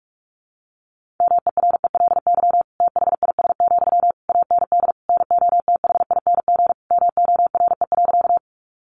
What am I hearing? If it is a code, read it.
Morse code, heard as a keyed tone, about 44 wpm, 700 Hz. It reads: MERELY THIS, AND NOTHING MORE.
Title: Sentences from Top 1400 Words 45wpm